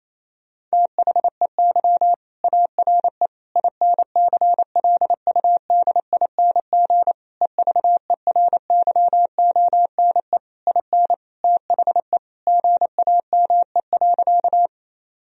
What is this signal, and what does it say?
Morse code, heard as a keyed tone, about 28 wpm, 700 Hz. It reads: THEY ARE INCLUDING EVERYONE IN THE GAME.
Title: Sentences from Top 1000 Words - Review Part 4 - 28wpm